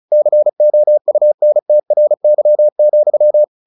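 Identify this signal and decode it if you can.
Morse code, heard as a keyed tone, about 36 wpm, 600 Hz. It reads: COUNTRY,
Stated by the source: Morse practice at 35 wpm from CW med Gnister, sending Ken Follett's novel Never in English.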